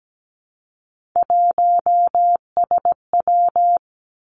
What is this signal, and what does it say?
Morse code, heard as a keyed tone, about 17 wpm, 700 Hz. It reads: 1SW
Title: Three Challenging Letter-Number Combinations 17wpm